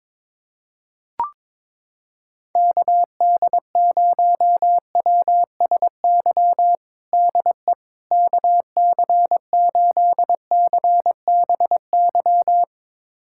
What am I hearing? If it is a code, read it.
Morse code, heard as a keyed tone, about 22 wpm, 700 Hz. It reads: KD0WSY DE KC8CBY
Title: USA Callsign DE Callsign 22wpm